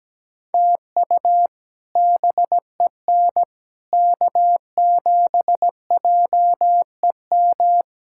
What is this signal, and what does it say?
Morse code, heard as a keyed tone, about 17 wpm, 700 Hz. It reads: TU BEN K7JEM
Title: K1USN SST QSO Practice - 17wpm